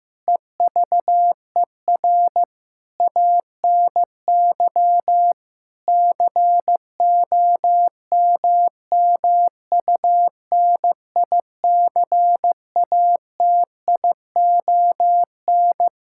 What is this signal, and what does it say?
Morse code, heard as a keyed tone, about 15 wpm, 700 Hz. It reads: EVER ANY COMMUNICATION
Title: Sentences from Top 1000 Words - Review Part 6 - 15wpm